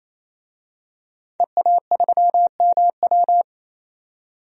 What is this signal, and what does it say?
Morse code, heard as a keyed tone, about 28 wpm, 700 Hz. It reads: EA3MW
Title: DXCC-100 Call Signs - 5 Characters - 28wpm